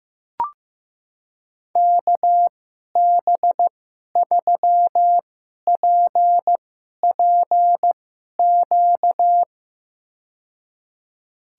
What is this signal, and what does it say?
Morse code, heard as a keyed tone, about 15 wpm, spaced as for 11 wpm, 700 Hz. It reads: KB3PPQ
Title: USA Call Signs - 2x Character Spacing - 15wpm